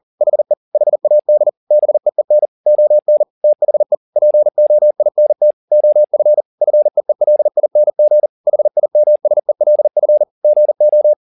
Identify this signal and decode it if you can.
Morse code, heard as a keyed tone, about 40 wpm, 600 Hz. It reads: HE HAD BEEN ON THE POINT OF FEELING HIMSELF GO
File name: the old man and the sea 2 40wpm